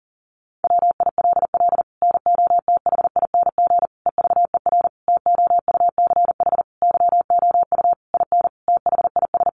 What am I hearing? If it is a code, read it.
Morse code, heard as a keyed tone, about 37 wpm, 700 Hz. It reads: WILL NOTHING EVER TOUCH YOU IN THIS
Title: Sentences from Top 1200 Words 40wpm